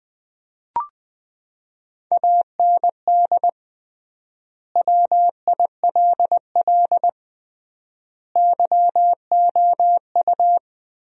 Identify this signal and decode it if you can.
Morse code, heard as a keyed tone, about 20 wpm, 700 Hz. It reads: AND WILL YOU
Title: 3x Word-Spacing - Sentences from Top 400 Words - 20wpm